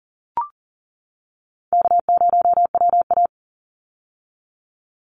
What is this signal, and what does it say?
Morse code, heard as a keyed tone, about 40 wpm, 700 Hz. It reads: K0WA
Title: Call Signs - 40wpm